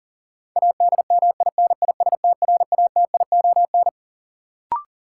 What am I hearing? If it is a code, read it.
Morse code, heard as a keyed tone, about 40 wpm, 700 Hz. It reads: ADMINISTRATION
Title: Sentences from Top 1300 Words - New Word Drill - 40wpm